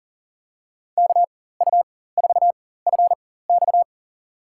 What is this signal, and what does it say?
Morse code, heard as a keyed tone, about 39 wpm, spaced as for 18 wpm, 700 Hz. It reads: KU4FX
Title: USA Call Signs - 5 Characters - 4x Character Spacing - 40wpm